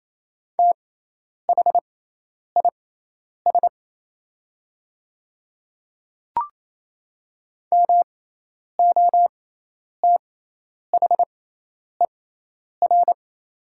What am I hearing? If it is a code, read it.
Morse code, heard as a keyed tone, about 28 wpm, spaced as for 10 wpm, 700 Hz. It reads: THIS MOTHER
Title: 6x Character Spacing - Top 500 Words - 28wpm